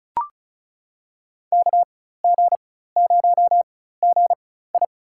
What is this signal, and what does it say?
Morse code, heard as a keyed tone, about 36 wpm, spaced as for 16 wpm, 700 Hz. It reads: KG0GI
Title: USA Call Signs - 5 Characters - 4x Character Spacing - 35wpm